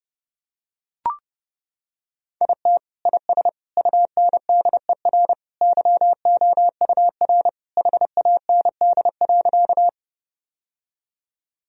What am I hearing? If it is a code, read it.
Morse code, heard as a keyed tone, about 30 wpm, 700 Hz. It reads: IT IS UNDER YOUR HAND.
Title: Sentences from Top 1000 Words - Review Part 1 - 30wpm